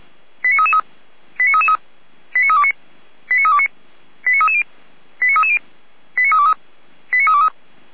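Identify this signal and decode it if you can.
Five-tone selective call (ZVEI), heard as quick runs of five tones, tones of about 80 ms. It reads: 89333 89333 89329 89329 89330 89330 89323 89323
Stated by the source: SDR-received ZVEI five-tone selective call.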